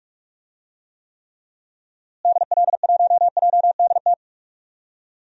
Morse code, heard as a keyed tone, about 46 wpm, 700 Hz. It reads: DL1JDT
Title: DXCC-100 Call Signs - 6 Characters - Repeated 3x - 45wpm